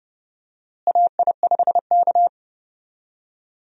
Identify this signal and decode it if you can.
Morse code, heard as a keyed tone, about 31 wpm, 700 Hz. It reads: AI5K
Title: USA Call Signs - 4 Characters - Repeated 2x - 30wpm